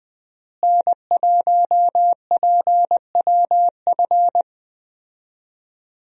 Morse code, heard as a keyed tone, about 20 wpm, 700 Hz. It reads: N1PWF
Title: USA Call Signs - 5 Characters - Repeated 3x - 20wpm